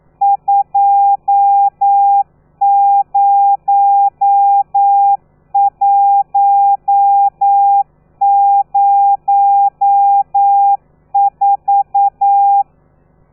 Morse code, heard as a keyed tone, about 9 wpm, 800 Hz. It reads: 20104
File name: telegrafia / i2tfj / Lez13 40